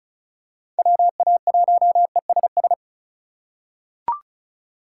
Morse code, heard as a keyed tone, about 36 wpm, 700 Hz. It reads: WA1ESS
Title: USA Call Signs - 6 Characters - 3x Character Spacing - 35wpm